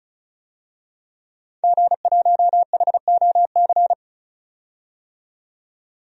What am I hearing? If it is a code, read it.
Morse code, heard as a keyed tone, about 36 wpm, 700 Hz. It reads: G1HOC
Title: DXCC-100 Call Signs - 5 Characters - 35wpm